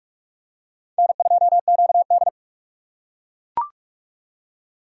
Morse code, heard as a keyed tone, about 42 wpm, 700 Hz. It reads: N2QD
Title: USA Call Signs - 4 Characters - 2x Character Spacing - 45wpm